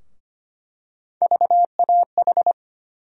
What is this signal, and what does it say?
Morse code, heard as a keyed tone, about 25 wpm, 700 Hz. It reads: VAH
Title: USA Call Sign Postfixes - 3 Letters 25wpm